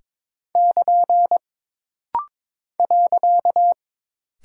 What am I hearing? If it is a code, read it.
Morse code, heard as a keyed tone, about 22 wpm, 700 Hz. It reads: <KN> .
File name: practice / 22wpm / Single Punctuation-Prosign - Rapid-Fire - 22wpm